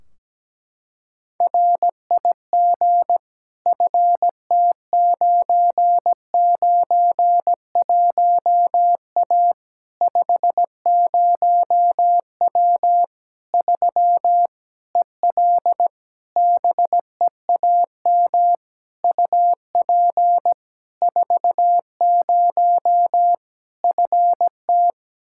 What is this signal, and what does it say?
Morse code, heard as a keyed tone, about 17 wpm, 700 Hz. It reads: RIG FT991A 50W 3 EL BEAM UP 40 FT
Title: Practice QSO - Exchange 2 - Info - 17wpm